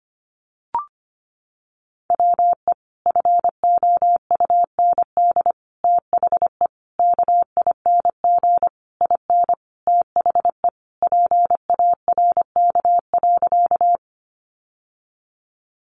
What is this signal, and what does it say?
Morse code, heard as a keyed tone, about 25 wpm, 700 Hz. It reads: WE FOUND THE KING IN THE PARK.
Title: Sentences from Top 1300 Words 25wpm